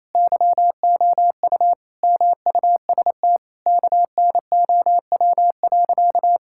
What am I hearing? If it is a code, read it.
Morse code, heard as a keyed tone, about 28 wpm, 700 Hz. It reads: YOU MUST KNOW.
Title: Sentences from Top 1000 Words - Review Part 6 - 28wpm